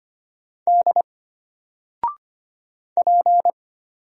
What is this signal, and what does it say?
Morse code, heard as a keyed tone, about 25 wpm, 700 Hz. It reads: D P
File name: Single-Letter-Number-25wpm